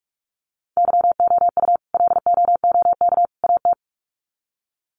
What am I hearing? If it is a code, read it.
Morse code, heard as a keyed tone, about 44 wpm, 700 Hz. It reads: YOU LOOK AT
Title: Sets of 3 Words 45wpm